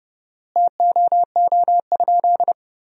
Morse code, heard as a keyed tone, about 32 wpm, 700 Hz. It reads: TOO?
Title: 4x Word-Spacing - Sentences from Top 500 Words - 30wpm